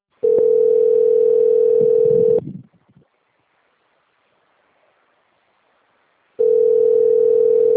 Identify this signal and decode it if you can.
Ringback tone (North America), 440 + 480 Hz.
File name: out- - - - - .23610